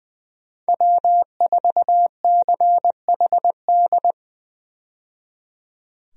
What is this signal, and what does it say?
Morse code, heard as a keyed tone, about 20 wpm, 700 Hz. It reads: W4CHD
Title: USA Call Signs - 5 Characters - 20wpm